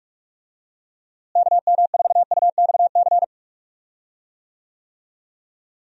Morse code, heard as a keyed tone, about 46 wpm, 700 Hz. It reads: KM4UXC
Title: USA Call Signs - 6 Characters - Repeated 2x - 45wpm